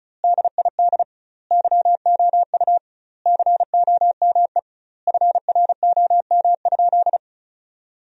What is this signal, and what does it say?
Morse code, heard as a keyed tone, about 36 wpm, 700 Hz. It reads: DID YOU COME FROM?
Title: 2x Word-Spacing - Sentences from Top 200 Words - 35wpm